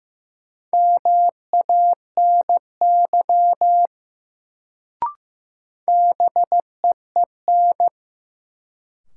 Morse code, heard as a keyed tone, about 15 wpm, 700 Hz. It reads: MANY BEEN
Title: Sets of 2 Words - New Word Drill - 15wpm